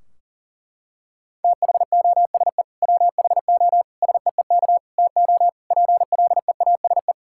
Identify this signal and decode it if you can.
Morse code, heard as a keyed tone, about 37 wpm, 700 Hz. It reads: THOSE WHO SEEK TO PLEASE
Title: Sentences from Top 1600 Words 40wpm